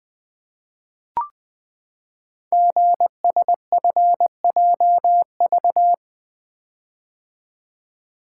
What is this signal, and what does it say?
Morse code, heard as a keyed tone, about 20 wpm, 700 Hz. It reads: GSFJV
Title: Code Groups - 5 Letters - 20wpm